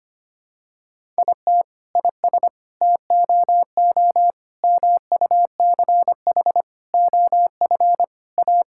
Morse code, heard as a keyed tone, about 25 wpm, 700 Hz. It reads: IT IS TOO MUCH OF A
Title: Sentences from Top 1000 Words - Review Part 5 - 25wpm